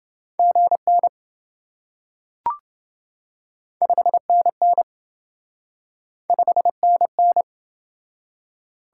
Morse code, heard as a keyed tone, about 26 wpm, 700 Hz. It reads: GN 5NN 5NN